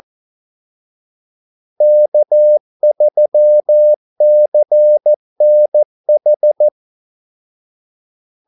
Morse code, heard as a keyed tone, about 14 wpm, 600 Hz. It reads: K3CNH
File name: Callsigns-14wpm